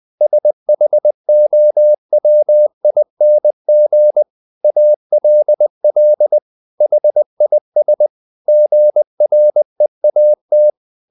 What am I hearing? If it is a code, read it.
Morse code, heard as a keyed tone, about 20 wpm, 600 Hz. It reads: SHOWING ALL HIS GREAT
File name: the old man and the sea part 2 20wpm